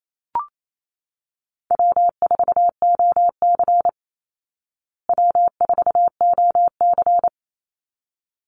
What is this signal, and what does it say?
Morse code, heard as a keyed tone, about 28 wpm, 700 Hz. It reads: W4OC W4OC
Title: Easy Call Signs Repeated 2x - 28wpm